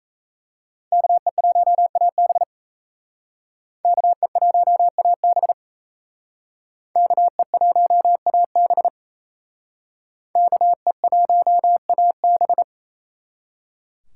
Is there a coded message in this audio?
Yes: Morse code from 0.9 to 12.6 s, heard as a keyed tone, about 37 wpm, 700 Hz. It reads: KE1AB KE1AB KE1AB KE1AB